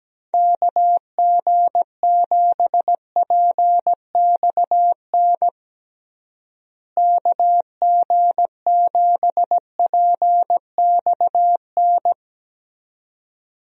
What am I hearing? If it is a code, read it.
Morse code, heard as a keyed tone, about 17 wpm, 700 Hz. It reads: KG7PXN KG7PXN
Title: USA Call Signs - 6 Characters - Repeated 2x - 17wpm